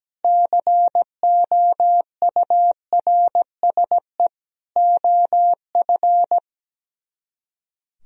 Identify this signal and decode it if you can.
Morse code, heard as a keyed tone, about 17 wpm, 700 Hz. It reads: COURSE OF
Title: Speed-Racing - Sets of 4 Words 17wpm